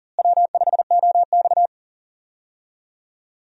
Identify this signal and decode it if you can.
Morse code, heard as a keyed tone, about 37 wpm, 700 Hz. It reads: W5OX
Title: USA Call Signs - 4 Characters - Repeated 2x - 40wpm